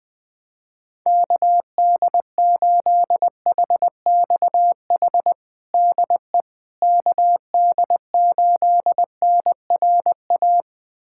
Morse code, heard as a keyed tone, about 20 wpm, 700 Hz. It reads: KD8HXH DE KD8NRA
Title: USA Callsign DE Callsign 20wpm